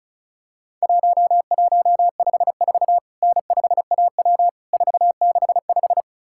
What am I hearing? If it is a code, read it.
Morse code, heard as a keyed tone, about 36 wpm, 700 Hz. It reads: 1154 N5AW 465
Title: Worked All Europe DX Contest - QTC Exchange - 35wpm